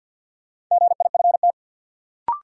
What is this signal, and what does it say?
Morse code, heard as a keyed tone, about 51 wpm, 700 Hz. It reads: GIFT
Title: Sentences from Top 900 Words - New Word Drill - 50wpm